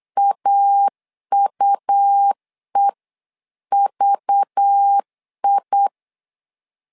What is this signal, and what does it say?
Morse code, heard as a keyed tone, about 8 wpm, 800 Hz. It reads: AUEVI
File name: LEZIONI / 5 LBVC / 5 LBV 8wpm 2837